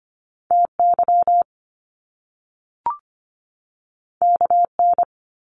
Morse code, heard as a keyed tone, about 25 wpm, 700 Hz. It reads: TY KN
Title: Top Two Letter Combinations in Words 25wpm